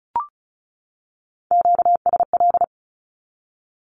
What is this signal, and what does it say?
Morse code, heard as a keyed tone, about 34 wpm, 700 Hz. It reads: QSL
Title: Q-Codes 35wpm